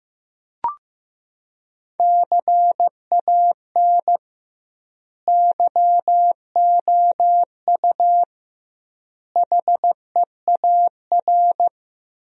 Morse code, heard as a keyed tone, about 15 wpm, 700 Hz. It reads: CAN YOU HEAR
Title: 2x Word-Spacing - Sentences from Top 300 Words - 15wpm